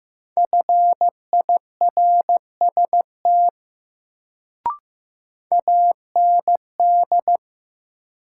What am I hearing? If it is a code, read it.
Morse code, heard as a keyed tone, about 15 wpm, 700 Hz. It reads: FIRST AND
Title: Sentences from Top 100 Words - New Word Drill - 15wpm